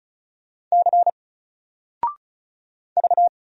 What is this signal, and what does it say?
Morse code, heard as a keyed tone, about 36 wpm, 700 Hz. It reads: C V
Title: Single Character (Letter:Number:Punctuation:Prosign) Alt 35wpm